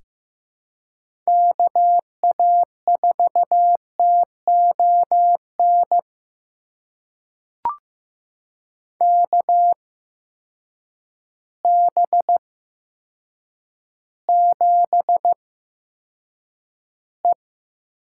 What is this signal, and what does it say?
Morse code, heard as a keyed tone, about 15 wpm, spaced as for 7 wpm, 700 Hz. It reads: KA4TON KB7E
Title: USA Call Signs - 6 Characters - 8x Character Spacing - 15wpm